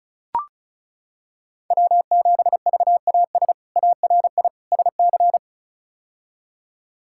Morse code, heard as a keyed tone, about 36 wpm, 700 Hz. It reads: W7VAS ARI SC
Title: K1USN SST QSO Practice - 35wpm